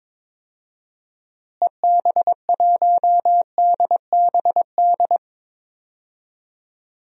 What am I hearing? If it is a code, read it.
Morse code, heard as a keyed tone, about 22 wpm, 700 Hz. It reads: EB1DBD
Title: DXCC-100 Call Signs - 6 Characters - Repeated 3x - 22wpm